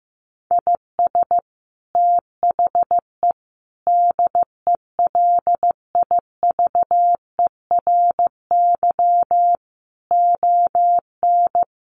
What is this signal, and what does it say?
Morse code, heard as a keyed tone, about 15 wpm, 700 Hz. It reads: IS THE DELIVERY ON
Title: Sentences from Top 1500 Words 15wpm